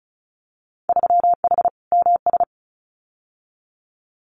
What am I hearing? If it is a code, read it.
Morse code, heard as a keyed tone, about 36 wpm, 700 Hz. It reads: 3H MS
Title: Winter Field Day Exchanges Encore 35wpm